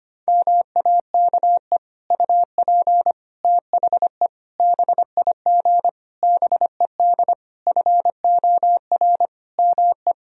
Morse code, heard as a keyed tone, about 25 wpm, 700 Hz. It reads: MAKE UP THE BIG BED FOR ME
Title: Sentences from Top 1000 Words - Review Part 1 - 25wpm